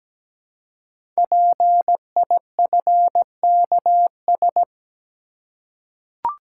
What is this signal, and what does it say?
Morse code, heard as a keyed tone, about 17 wpm, 700 Hz. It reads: PIFKS
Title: Code Groups - 5 Letters - 17wpm